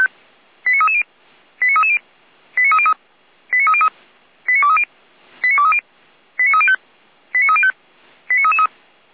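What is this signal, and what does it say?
Five-tone selective call (ZVEI), heard as quick runs of five tones, tones of about 70 ms. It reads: 89330 89330 89303 89303 89230 89230 89305 89305
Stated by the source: ZVEI five-tone call 89319 received on an SDR, from feuerwehr-orlamuende.org (fire brigade).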